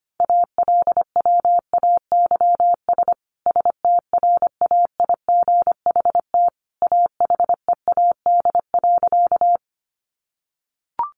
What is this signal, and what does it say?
Morse code, heard as a keyed tone, about 25 wpm, 700 Hz. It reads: ALWAYS STRAIGHT AHEAD.
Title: Sentences from Top 1100 Words 25wpm